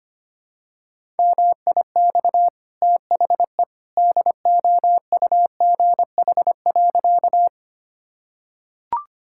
Morse code, heard as a keyed tone, about 25 wpm, 690 Hz. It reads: MIX THE DOUGH.
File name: Easy Sentences with Hints - 25wpm